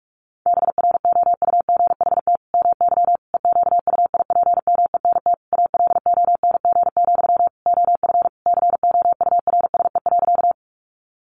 Morse code, heard as a keyed tone, about 44 wpm, 700 Hz. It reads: BROUGHT MY EQUIPMENT ALONG, OF COURSE.
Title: Sentences from Top 700 Words 45wpm